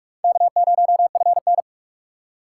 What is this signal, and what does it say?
Morse code, heard as a keyed tone, about 46 wpm, 700 Hz. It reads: K0FN
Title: USA Call Signs - 6x Character Spacing - 45wpm